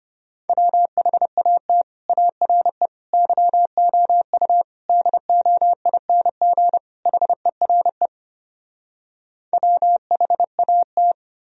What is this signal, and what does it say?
Morse code, heard as a keyed tone, about 33 wpm, 700 Hz. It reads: WHAT ARE YOU DOING HERE WHAT